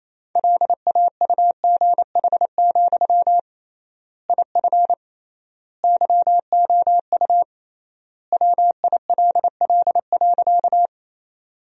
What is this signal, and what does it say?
Morse code, heard as a keyed tone, about 28 wpm, 700 Hz. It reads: LAUGH, IF YOU WILL.
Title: 3x Word-Spacing - Sentences from Top 500 Words - 28wpm